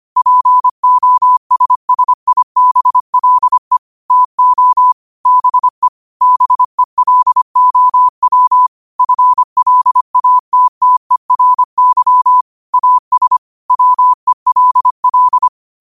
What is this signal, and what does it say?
Morse code, heard as a keyed tone, about 25 wpm, 1000 Hz. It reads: POSSIBLE TO BE BELOW FLATTERY AS WELL
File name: qotdfiles / 25wpm